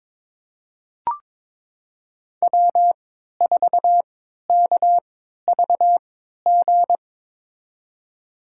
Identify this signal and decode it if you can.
Morse code, heard as a keyed tone, about 22 wpm, spaced as for 13 wpm, 700 Hz. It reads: W4KVG
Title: USA Call Signs - 5 Characters - 3x Character Spacing - 22wpm